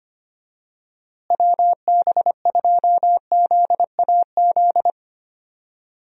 Morse code, heard as a keyed tone, about 25 wpm, 700 Hz. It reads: WB2ZAZ
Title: USA Call Signs - 6 Characters - Repeated 3x - 25wpm